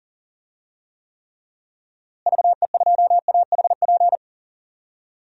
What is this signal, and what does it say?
Morse code, heard as a keyed tone, about 40 wpm, 700 Hz. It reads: VE2AHP
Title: DXCC-100 Call Signs - 6 Characters - Repeated 2x - 40wpm